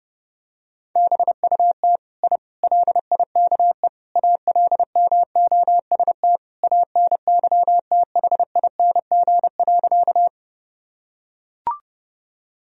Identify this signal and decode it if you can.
Morse code, heard as a keyed tone, about 30 wpm, 700 Hz. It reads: BUT I LIKE ALMOST ANYTHING.
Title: Sentences from Top 1000 Words - Review Part 4 - 30wpm